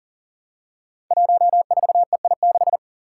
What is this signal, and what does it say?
Morse code, heard as a keyed tone, about 39 wpm, 700 Hz. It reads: 14EI6
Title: Five Challenging Letter-Number Combinations 40wpm